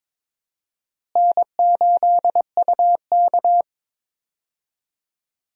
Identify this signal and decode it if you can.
Morse code, heard as a keyed tone, about 22 wpm, 700 Hz. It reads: N8UK
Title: USA Call Signs - 4 Characters - Repeated 3x - 22wpm